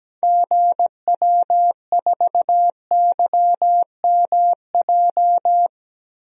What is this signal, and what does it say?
Morse code, heard as a keyed tone, about 17 wpm, 700 Hz. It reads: GW4YMJ
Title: DXCC-100 Call Signs - 6 Characters - 17wpm